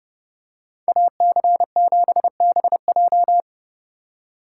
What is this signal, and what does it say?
Morse code, heard as a keyed tone, about 29 wpm, 700 Hz. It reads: AC7BJ